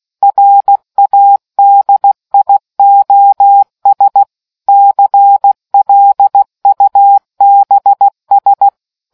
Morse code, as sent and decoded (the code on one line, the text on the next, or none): .-. .- -.. .. --- ... / -.-. .-.. ..- -... ...
RADIOS CLUBS